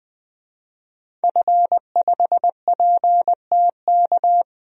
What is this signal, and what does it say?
Morse code, heard as a keyed tone, about 20 wpm, 700 Hz. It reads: F5PTK
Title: DXCC-100 Call Signs - 5 Characters - Repeated 2x - 20wpm